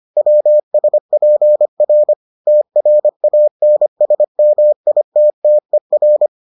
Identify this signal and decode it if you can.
Morse code, heard as a keyed tone, about 25 wpm, 600 Hz. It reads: WSPR TRANSMITTER